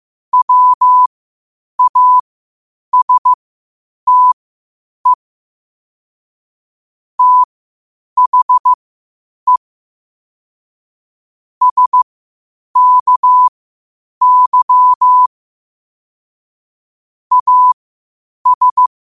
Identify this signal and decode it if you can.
Morse code, heard as a keyed tone, about 15 wpm, spaced as for 8 wpm, 1000 Hz. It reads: WASTE THE SKY AS
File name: qotdfiles / 8wpm